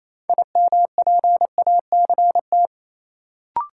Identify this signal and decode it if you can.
Morse code, heard as a keyed tone, about 28 wpm, 700 Hz. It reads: IMPACT
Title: Sentences from Top 1000 Words - Review Part 6 - New Word Drill - 28wpm